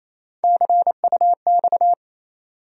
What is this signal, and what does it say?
Morse code, heard as a keyed tone, about 28 wpm, 700 Hz. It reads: CUX
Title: USA Call Sign Postfixes - 3 Letters 28wpm